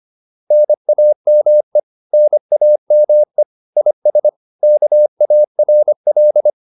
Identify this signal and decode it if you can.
Morse code, heard as a keyed tone, about 25 wpm, 600 Hz. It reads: NAME NAME IS KARL